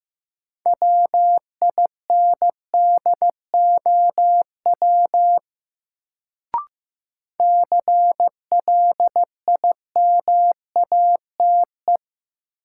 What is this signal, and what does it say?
Morse code, heard as a keyed tone, about 15 wpm, 700 Hz. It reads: WINDOW CLIMATE
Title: Sentences from Top 1600 Words - New Word Drill Alt - 15wpm